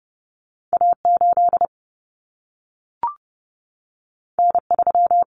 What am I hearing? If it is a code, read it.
Morse code, heard as a keyed tone, about 30 wpm, 700 Hz. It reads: A8 N3
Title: Common Cut Numbers - 2-Digit 30wpm